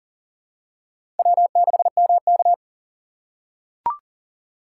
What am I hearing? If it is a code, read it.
Morse code, heard as a keyed tone, about 40 wpm, 700 Hz. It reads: W6MK
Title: USA Call Signs - 4 Characters - 2x Character Spacing - 40wpm